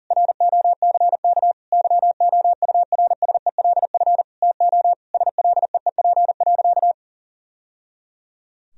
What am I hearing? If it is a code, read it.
Morse code, heard as a keyed tone, about 40 wpm, 700 Hz. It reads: ROCK YOURSELF TO SLEEP.
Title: Sentences from Top 1000 Words - Review Part 2 - 40wpm